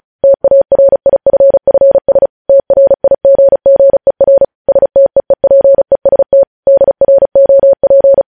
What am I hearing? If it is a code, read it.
Morse code, heard as a keyed tone, about 36 wpm, 550 Hz. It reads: TARIFFS TRIGGER STEEPEST DROP
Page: news headlines in morse code at 35 wpm